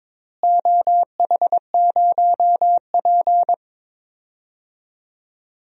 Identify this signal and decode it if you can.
Morse code, heard as a keyed tone, about 22 wpm, 700 Hz. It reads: OH0P
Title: DXCC-100 Call Signs - 4 Characters - Repeated 2x - 22wpm